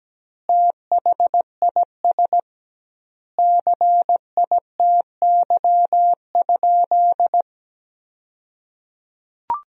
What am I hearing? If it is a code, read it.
Morse code, heard as a keyed tone, about 17 wpm, 700 Hz. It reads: THIS CITY?
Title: 2x Word-Spacing - Sentences from Top 300 Words - 17wpm